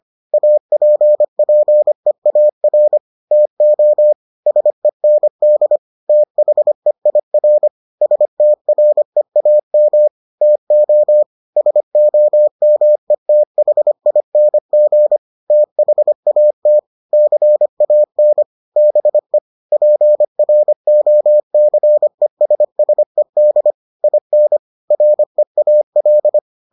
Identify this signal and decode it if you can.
Morse code, heard as a keyed tone, about 25 wpm, 600 Hz. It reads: APPEAR TO SEND THEIR STREAM TO SOMETHING THAT CAN BE PROCESSED IN REAL